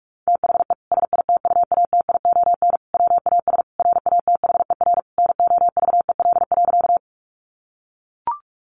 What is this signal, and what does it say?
Morse code, heard as a keyed tone, about 44 wpm, 700 Hz. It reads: THE SITUATION WAS RATHER NOVEL.
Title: Sentences from Top 1300 Words 45wpm